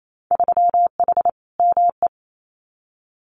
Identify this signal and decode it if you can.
Morse code, heard as a keyed tone, about 28 wpm, 700 Hz. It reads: 3H ME
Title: Winter Field Day Exchanges Encore 28wpm